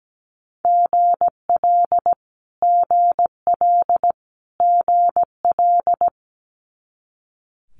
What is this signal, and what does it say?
Morse code, heard as a keyed tone, about 17 wpm, 700 Hz. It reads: GL GL GL
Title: Multi-letter Phonemes Repeated 3x 17wpm